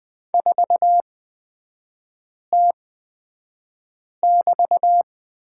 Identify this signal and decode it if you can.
Morse code, heard as a keyed tone, about 20 wpm, spaced as for 5 wpm, 700 Hz. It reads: 4T<BT>
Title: 236 Lesson 38e - Letter q - Three Characters 20wpm